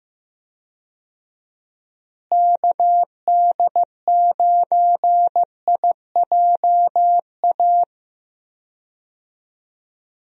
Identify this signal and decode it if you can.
Morse code, heard as a keyed tone, about 15 wpm, 700 Hz. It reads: KD9IJA
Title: USA Call Signs - 6 Characters - Repeated 3x - 15wpm